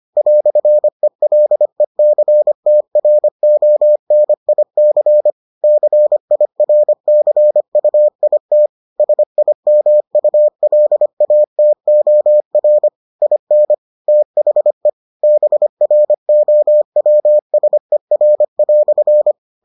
Morse code, heard as a keyed tone, about 25 wpm, 600 Hz. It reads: "ELECTRONIC CIRCUIT SIMULATOR IN THE BROWSER"